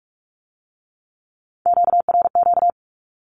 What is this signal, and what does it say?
Morse code, heard as a keyed tone, about 46 wpm, 700 Hz. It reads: QRQ
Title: Q-Codes with Spoken Meaning - 45wpm